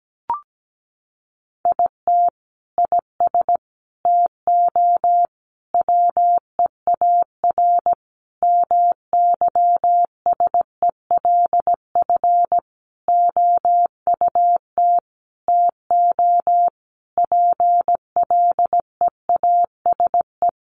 Morse code, heard as a keyed tone, about 17 wpm, 700 Hz. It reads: IT IS TO WEAR MYSELF OUT TO PLEASE
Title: Sentences from Top 1500 Words 17wpm